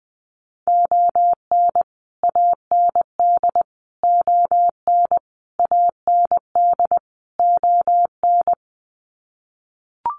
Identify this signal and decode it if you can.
Morse code, heard as a keyed tone, about 20 wpm, 700 Hz. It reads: ON AND ON AND ON
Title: Sets of 5 Words Encore2 20wpm